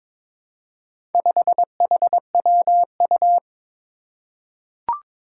Morse code, heard as a keyed tone, about 22 wpm, 700 Hz. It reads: 5HWU
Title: Four Challenging Letter-Number Combinations 22wpm